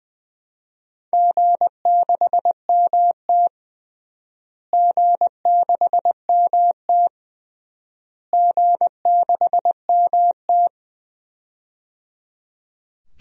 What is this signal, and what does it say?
Morse code, heard as a keyed tone, about 21 wpm, 700 Hz. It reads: G6MT G6MT G6MT
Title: DXCC-100 Call Signs - 4 Characters - Repeated 3x - 20wpm